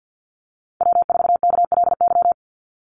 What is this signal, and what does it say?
Morse code, heard as a keyed tone, about 50 wpm, 700 Hz. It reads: W4KLY
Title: Call Signs - 50wpm